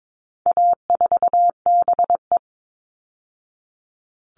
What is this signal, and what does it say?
Morse code, heard as a keyed tone, about 22 wpm, 700 Hz. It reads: A4BE
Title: All Cut Numbers - 4-Digit 22wpm